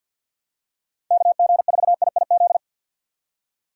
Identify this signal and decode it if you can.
Morse code, heard as a keyed tone, about 53 wpm, 700 Hz. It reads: KG4IIZ
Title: USA Call Signs - 6 Characters - 4x Character Spacing - 50wpm